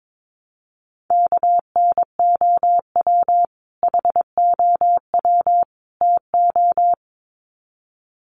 Morse code, heard as a keyed tone, about 22 wpm, 700 Hz. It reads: KNOW HOW TO
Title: Sets of 3 Words 22wpm